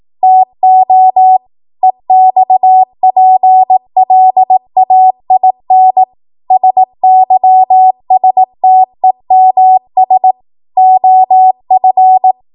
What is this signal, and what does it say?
Morse code, heard as a keyed tone, about 18 wpm, 750 Hz. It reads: TO EXPLAIN SYSTEMS OF